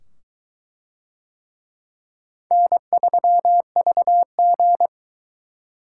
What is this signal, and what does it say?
Morse code, heard as a keyed tone, about 22 wpm, 700 Hz. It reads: N3VG